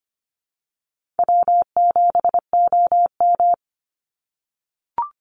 Morse code, heard as a keyed tone, about 25 wpm, 700 Hz. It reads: W7OM
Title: Easy Call Signs - 25wpm